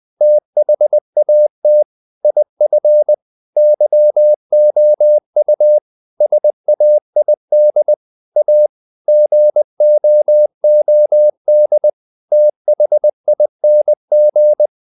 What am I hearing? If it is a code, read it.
Morse code, heard as a keyed tone, about 20 wpm, 600 Hz. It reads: THAT IF YOU SAID A GOOD THING
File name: the old man and the sea / the old man and the sea part 1 20wpm